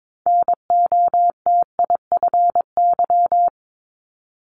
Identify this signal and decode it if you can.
Morse code, heard as a keyed tone, about 22 wpm, 700 Hz. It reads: NOTIFY
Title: Multi-letter Phonemes Repeated 3x 22wpm